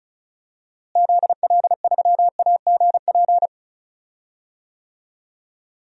Morse code, heard as a keyed tone, about 34 wpm, 700 Hz. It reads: ZL3AGP
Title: DXCC-100 Call Signs - 6 Characters - Repeated 2x - 35wpm